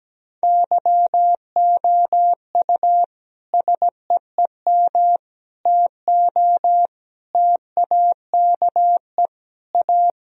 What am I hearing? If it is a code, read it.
Morse code, heard as a keyed tone, about 17 wpm, 700 Hz. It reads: YOU SEEM TO TAKE A
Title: Sentences from Top 1000 Words - Review Part 3 - 17wpm